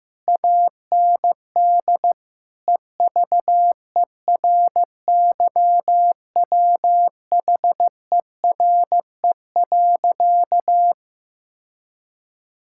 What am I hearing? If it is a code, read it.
Morse code, heard as a keyed tone, about 15 wpm, 700 Hz. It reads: AND EVERYWHERE.
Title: Sentences from Top 1600 Words 15wpm